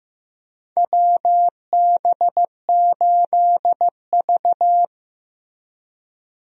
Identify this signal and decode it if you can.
Morse code, heard as a keyed tone, about 15 wpm, 700 Hz. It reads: WB8V
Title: USA Call Signs - 4 Characters - Repeated 3x - 15wpm